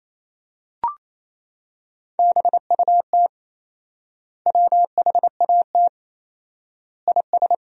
Morse code, heard as a keyed tone, about 28 wpm, 700 Hz. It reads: BUT WHAT IS
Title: 4x Word-Spacing - Sentences from Top 300 Words - 28wpm